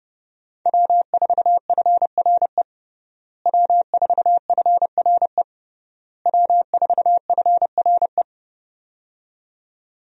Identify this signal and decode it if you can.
Morse code, heard as a keyed tone, about 31 wpm, 700 Hz. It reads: W4FRE W4FRE W4FRE
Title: USA Call Signs - 5 Characters - Repeated 3x - 30wpm